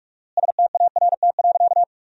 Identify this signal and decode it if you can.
Morse code, heard as a keyed tone, about 46 wpm, 700 Hz. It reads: START.
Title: 4x Word-Spacing - Sentences from Top 300 Words - 45wpm